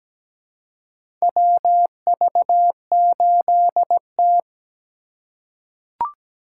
Morse code, heard as a keyed tone, about 17 wpm, 700 Hz. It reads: WV8T
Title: USA Call Signs - 3x Character Spacing - 17wpm